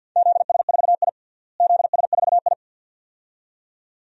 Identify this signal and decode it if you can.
Morse code, heard as a keyed tone, about 53 wpm, 700 Hz. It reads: ZS4I ZS4I
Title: DXCC-100 Call Signs - 4 Characters - Repeated 2x - 50wpm